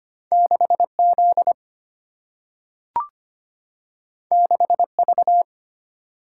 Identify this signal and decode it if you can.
Morse code, heard as a keyed tone, about 25 wpm, 700 Hz. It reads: 6 Z 6 V
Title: Single-Letter Single-Number 25wpm